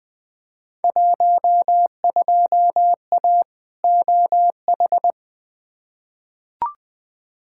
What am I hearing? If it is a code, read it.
Morse code, heard as a keyed tone, about 20 wpm, 700 Hz. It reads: 12A OH
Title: Field Day Exchanges Encore - 20wpm